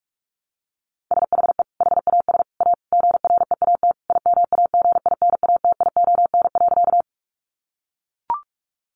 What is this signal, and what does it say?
Morse code, heard as a keyed tone, about 44 wpm, 700 Hz. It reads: SHE HAS A GREAT IMAGINATION.
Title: Sentences from Top 1300 Words 45wpm